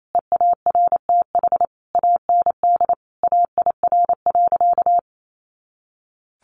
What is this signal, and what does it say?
Morse code, heard as a keyed tone, about 28 wpm, 700 Hz. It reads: EARTH AND AIR.
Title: Sentences from Top 400 Words 28wpm